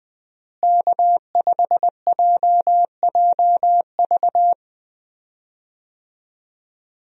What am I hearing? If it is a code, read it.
Morse code, heard as a keyed tone, about 20 wpm, 700 Hz. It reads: K5JJV
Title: USA Call Signs - 5 Characters - Repeated 2x - 20wpm